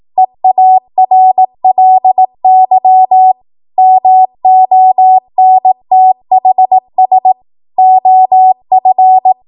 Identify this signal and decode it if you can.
Morse code, heard as a keyed tone, about 18 wpm, 750 Hz. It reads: EARLY MONTHS OF
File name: Morse / 18WPM